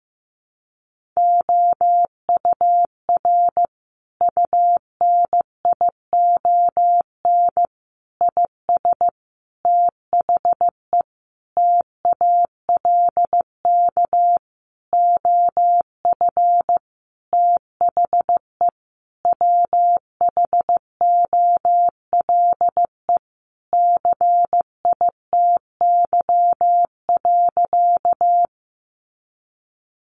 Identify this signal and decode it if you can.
Morse code, heard as a keyed tone, about 14 wpm, 700 Hz. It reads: OUR UNION IS THE TALK OF THE WHOLE CITY.
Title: Sentences from Top 1400 Words 15wpm